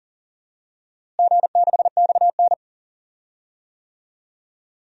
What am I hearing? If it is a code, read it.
Morse code, heard as a keyed tone, about 40 wpm, 700 Hz. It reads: G6XN
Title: DXCC-100 Call Signs - 4 Characters - Repeated 2x - 40wpm